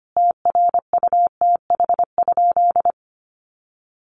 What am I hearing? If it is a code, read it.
Morse code, heard as a keyed tone, about 25 wpm, 700 Hz. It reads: TRUTH?
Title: Sentences from Top 1100 Words 25wpm